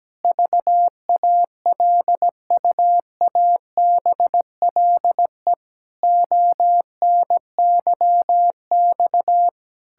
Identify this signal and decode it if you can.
Morse code, heard as a keyed tone, about 17 wpm, 700 Hz. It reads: VALUABLE ONYX
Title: Pangram Challenge v2 - 17wpm